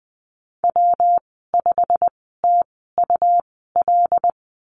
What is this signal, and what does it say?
Morse code, heard as a keyed tone, about 20 wpm, spaced as for 15 wpm, 700 Hz. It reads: W5TUL
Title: 173 Lesson 28g - Number 9 - Call Signs 20wpm